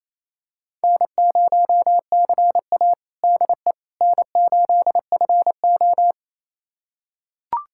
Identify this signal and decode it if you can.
Morse code, heard as a keyed tone, about 27 wpm, 690 Hz. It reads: N0CA DE N8FO
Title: USA Callsign DE Callsign - 4 Characters - 28wpm